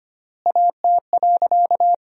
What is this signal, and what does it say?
Morse code, heard as a keyed tone, about 25 wpm, 700 Hz. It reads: AT.
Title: 4x Word-Spacing - Sentences from Top 200 Words - 25wpm